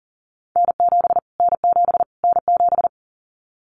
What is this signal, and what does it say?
Morse code, heard as a keyed tone, about 40 wpm, 700 Hz. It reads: N7 N7 N7
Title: Callsign Prefixes for Top 25 DXCC - Repeated 3x - 40wpm